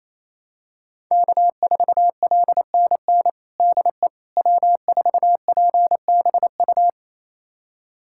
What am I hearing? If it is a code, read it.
Morse code, heard as a keyed tone, about 28 wpm, 700 Hz. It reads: K4LNN DE W4PBU
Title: USA Callsign DE Callsign - 5 Characters - 28wpm